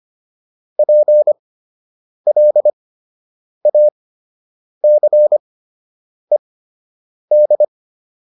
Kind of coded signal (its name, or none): Morse code